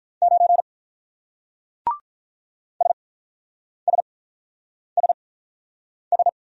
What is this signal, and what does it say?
Morse code, heard as a keyed tone, about 44 wpm, spaced as for 9 wpm, 700 Hz. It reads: 9 SSSS